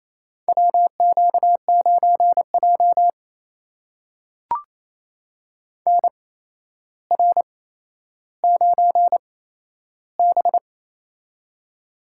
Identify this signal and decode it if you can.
Morse code, heard as a keyed tone, about 29 wpm, spaced as for 12 wpm, 700 Hz. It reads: WQ9J NR9B
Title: USA Call Signs - 4 Characters - 8x Character Spacing - 28wpm